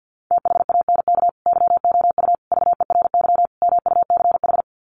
Morse code, heard as a keyed tone, about 50 wpm, 700 Hz. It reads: THANK YOU VERY MUCH
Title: Sets of 4 Words 50wpm